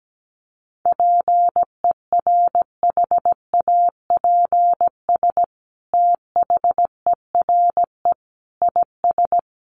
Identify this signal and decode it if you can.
Morse code, heard as a keyed tone, about 17 wpm, 700 Hz. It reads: PERHAPS THERE IS
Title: Sentences from Top 600 Words 17wpm